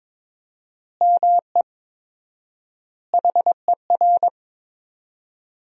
Morse code, heard as a keyed tone, about 22 wpm, 700 Hz. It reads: ME HER
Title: 4x Word-Spacing - Sentences from Top 500 Words - 22wpm